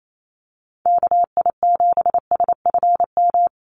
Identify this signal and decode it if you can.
Morse code, heard as a keyed tone, about 28 wpm, 700 Hz. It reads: KI7SFM
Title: USA Call Signs Repeated 3x 28wpm